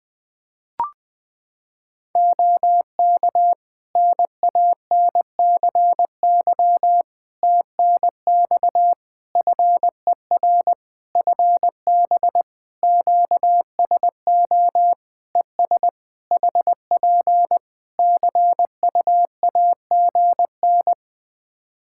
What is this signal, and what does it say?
Morse code, heard as a keyed tone, about 20 wpm, 700 Hz. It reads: OK NANCY TNX FER FB QSO ES HP CU AGN
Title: Practice QSO - Exchange 4 - Ending - 20wpm